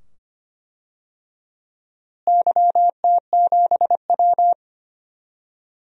Morse code, heard as a keyed tone, about 25 wpm, 700 Hz. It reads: YT7W
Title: DXCC-100 Call Signs - 4 Characters - Repeated 2x - 25wpm